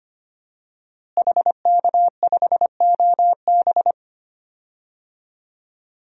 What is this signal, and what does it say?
Morse code, heard as a keyed tone, about 25 wpm, 700 Hz. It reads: HK5OB
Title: DXCC-100 Call Signs - 5 Characters - Repeated 2x - 25wpm